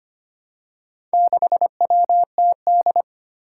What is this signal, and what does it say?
Morse code, heard as a keyed tone, about 25 wpm, 700 Hz. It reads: 6WTD
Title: Four Challenging Letter-Number Combinations 25wpm